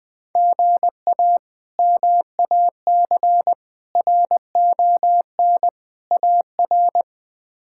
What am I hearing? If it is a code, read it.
Morse code, heard as a keyed tone, about 20 wpm, 700 Hz. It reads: GA MAC RON AR
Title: K1USN SST QSO Practice - 20wpm